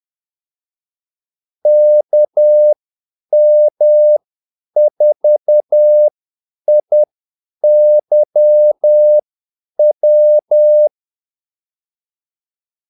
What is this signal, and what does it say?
Morse code, heard as a keyed tone, about 10 wpm, spaced as for 8 wpm, 600 Hz. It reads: KM4IYW
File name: Callsigns-10-8wpm